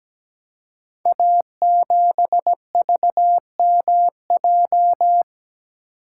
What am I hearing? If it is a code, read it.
Morse code, heard as a keyed tone, about 17 wpm, 700 Hz. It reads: A7VMJ
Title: Code Groups - 5 Letter:Numbers - 17wpm